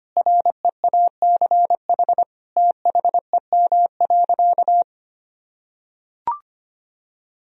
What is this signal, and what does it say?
Morse code, heard as a keyed tone, about 25 wpm, 700 Hz. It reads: REACH THEM.
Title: Sentences from Top 1000 Words - Review Part 2 - 25wpm